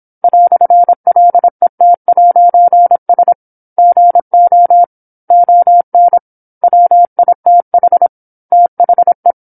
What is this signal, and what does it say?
Morse code, heard as a keyed tone, about 26 wpm, 700 Hz. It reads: "LET'S GO ON WITH THE
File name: 08-Chapter-08 26 WPM